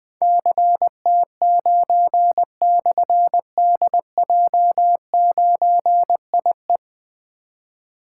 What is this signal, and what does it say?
Morse code, heard as a keyed tone, about 20 wpm, 700 Hz. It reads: CT9/DJ9IE
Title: Difficult Call Signs 20wpm